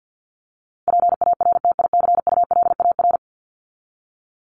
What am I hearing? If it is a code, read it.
Morse code, heard as a keyed tone, about 50 wpm, 700 Hz. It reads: PARTICULAR
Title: LY Suffix - Top 100 Words with Base Words 50wpm